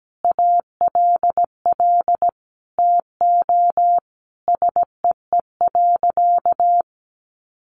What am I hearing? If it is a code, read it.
Morse code, heard as a keyed tone, about 17 wpm, 700 Hz. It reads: ALL TO SEE.
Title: Sentences from Top 1200 Words 17wpm